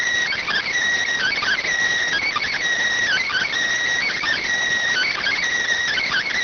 Emergency Alert System (SAME) tones (header bursts), heard as harsh buzzing bursts. It is heard from 1.1 to 5.4 s.
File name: findings / CT 205 ax mips, iDose 3 0004.gen-32 rx